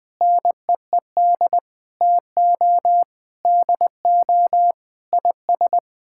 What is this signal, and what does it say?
Morse code, heard as a keyed tone, about 20 wpm, 700 Hz. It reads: NEED TO DO IS
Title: Sets of 8 Words - 20wpm